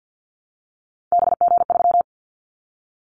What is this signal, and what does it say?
Morse code, heard as a keyed tone, about 50 wpm, 700 Hz. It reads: BG3
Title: Callsign Prefixes for Top 100 DXCC - 50wpm